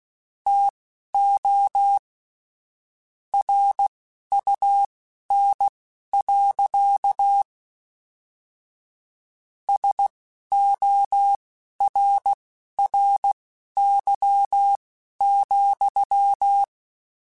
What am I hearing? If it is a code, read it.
Morse code, heard as a keyed tone, about 16 wpm, spaced as for 11 wpm, 800 Hz. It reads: TO RUN. SORRY,
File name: QSOs / QSOs-mp3s / kc7haa@10